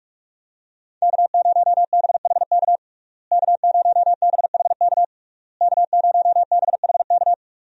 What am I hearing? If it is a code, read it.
Morse code, heard as a keyed tone, about 42 wpm, 700 Hz. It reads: K0BHK K0BHK K0BHK
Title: USA Call Signs - 5 Characters - Repeated 3x - 45wpm